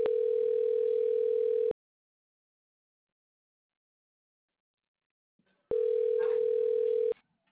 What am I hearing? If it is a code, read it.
Ringback tone (North America), 440 + 480 Hz.